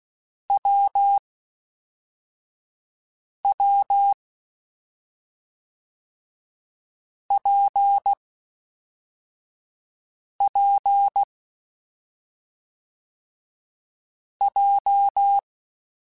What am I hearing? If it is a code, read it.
Morse code, heard as a keyed tone, about 16 wpm, 800 Hz. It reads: W W P P J
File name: Difficult Combos / Difficult Combos-mp3s / WPJ WPJ1 @5